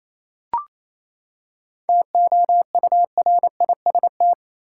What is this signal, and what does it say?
Morse code, heard as a keyed tone, about 28 wpm, 700 Hz. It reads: TOURIST
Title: Seven Letter Words - 28wpm